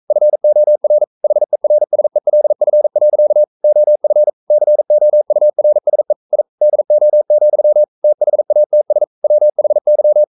Morse code, heard as a keyed tone, about 42 wpm, 600 Hz. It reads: FOR HERSELF. OF COURSE I DO, THATS WHY
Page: Romanen Never af Ken Follet på engelsk.